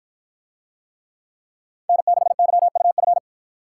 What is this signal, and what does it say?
Morse code, heard as a keyed tone, about 53 wpm, 700 Hz. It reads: N6YUF